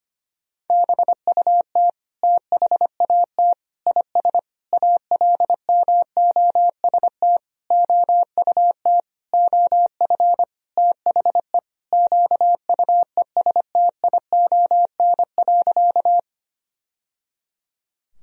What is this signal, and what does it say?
Morse code, heard as a keyed tone, about 25 wpm, 700 Hz. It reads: BUT THAT IS ALMOST OUT OF THE QUESTION.
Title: Sentences from Top 1000 Words - Review Part 4 - 25wpm